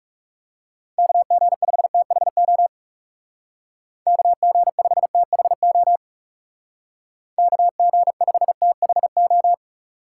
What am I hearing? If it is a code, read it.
Morse code, heard as a keyed tone, about 35 wpm, 700 Hz. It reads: KG5THO KG5THO KG5THO
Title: Speed Racing - USA 2x3 Call Signs 30wpm